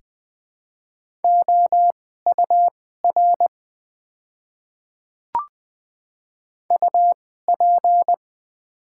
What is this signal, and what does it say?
Morse code, heard as a keyed tone, about 20 wpm, spaced as for 14 wpm, 700 Hz. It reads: OUR UP
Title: 2x Character Spacing - Top 500 Words - 20wpm